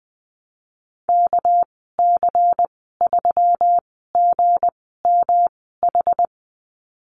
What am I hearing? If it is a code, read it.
Morse code, heard as a keyed tone, about 20 wpm, spaced as for 15 wpm, 700 Hz. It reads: KC3GMH
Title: 231 Lesson 37g - Letter x - Call Signs 20wpm